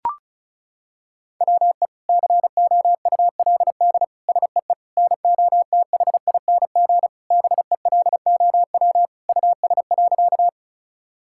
Morse code, heard as a keyed tone, about 34 wpm, 700 Hz. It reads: WE COULD SEE NOTHING BELOW US.
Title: Sentences from Top 1000 Words - Review Part 5 - 35wpm